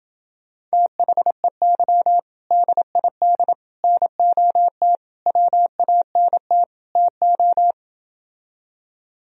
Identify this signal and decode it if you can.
Morse code, heard as a keyed tone, about 29 wpm, 700 Hz. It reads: THEY DID NOT WANT TO